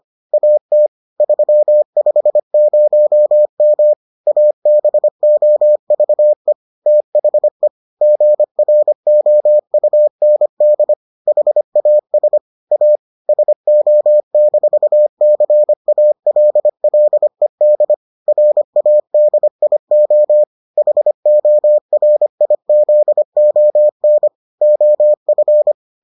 Morse code, heard as a keyed tone, about 25 wpm, 600 Hz. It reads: AT 350M ABOVE THE GROUND HAS A SO-CALLED RADIO HORIZON OF